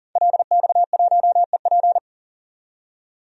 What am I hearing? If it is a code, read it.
Morse code, heard as a keyed tone, about 40 wpm, 700 Hz. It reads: LX1EP
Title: DXCC-100 Call Signs - 5 Characters - 40wpm